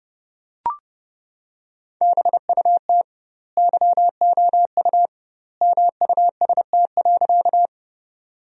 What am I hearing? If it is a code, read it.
Morse code, heard as a keyed tone, about 31 wpm, 700 Hz. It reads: BUT YOU MUST.
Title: 2x Word-Spacing - Sentences from Top 200 Words - 30wpm